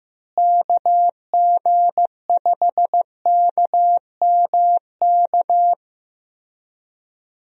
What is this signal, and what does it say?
Morse code, heard as a keyed tone, about 15 wpm, 700 Hz. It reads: KG5KMK
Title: USA Call Signs - 6 Characters - Repeated 2x - 15wpm